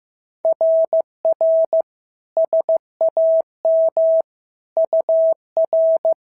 Morse code, heard as a keyed tone, about 14 wpm, 650 Hz. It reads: RR SAM UR
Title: SOTA QSO Practice 15wpm